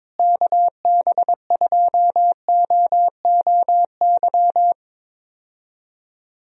Morse code, heard as a keyed tone, about 22 wpm, 700 Hz. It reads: KB2OOY
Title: USA Call Signs - 6 Characters - Repeated 2x - 22wpm